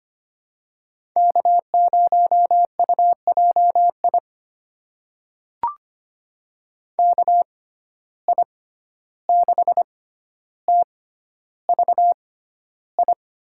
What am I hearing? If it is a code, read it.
Morse code, heard as a keyed tone, about 25 wpm, spaced as for 12 wpm, 700 Hz. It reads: K0UJI KI6TVI
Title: USA Call Signs - 6x Character Spacing - 25wpm